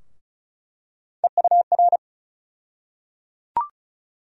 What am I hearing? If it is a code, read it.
Morse code, heard as a keyed tone, about 36 wpm, 700 Hz. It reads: EUR
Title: USA Call Sign Postfixes - 3 Letters 35wpm